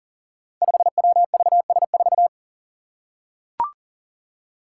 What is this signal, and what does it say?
Morse code, heard as a keyed tone, about 40 wpm, 700 Hz. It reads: 5WVS4
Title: Five Challenging Letter-Number Combinations 40wpm